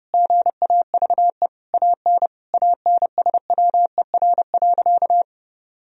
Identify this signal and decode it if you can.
Morse code, heard as a keyed tone, about 30 wpm, 700 Hz. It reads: GAVE AN ANSWER.
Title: Sentences from Top 1000 Words - Review Part 4 - 30wpm